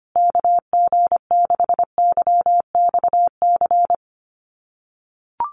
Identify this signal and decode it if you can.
Morse code, heard as a keyed tone, about 25 wpm, 700 Hz. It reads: KG6YXC
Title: USA Call Signs Repeated 2x 25wpm